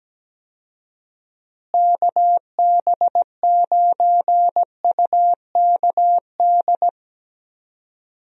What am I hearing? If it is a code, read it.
Morse code, heard as a keyed tone, about 17 wpm, 700 Hz. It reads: KB9UKD
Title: USA Call Signs - 6 Characters - Repeated 3x - 17wpm